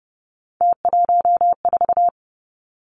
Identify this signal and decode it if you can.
Morse code, heard as a keyed tone, about 30 wpm, 700 Hz. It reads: T14
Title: Three Challenging Letter-Number Combinations 30wpm